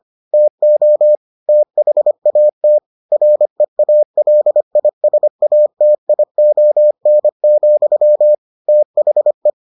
Morse code, heard as a keyed tone, about 25 wpm, 600 Hz. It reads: TO THAT REALISATION, THE